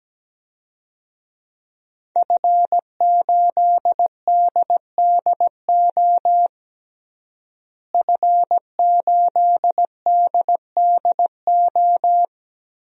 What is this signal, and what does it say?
Morse code, heard as a keyed tone, about 17 wpm, 700 Hz. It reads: F8DDO F8DDO
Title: DXCC-100 Call Signs - 5 Characters - Repeated 3x - 17wpm